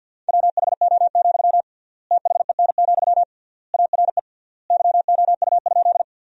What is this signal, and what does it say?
Morse code, heard as a keyed tone, about 50 wpm, 700 Hz. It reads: WHO, THEN, ARE YOU?
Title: 3x Word-Spacing - Sentences from Top 100 Words - 50wpm